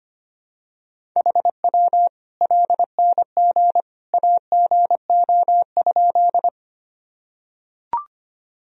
Morse code, heard as a keyed tone, about 25 wpm, 700 Hz. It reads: HW LNG AGO?
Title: Sentences from Top 500 Words and Abbreviations 25wpm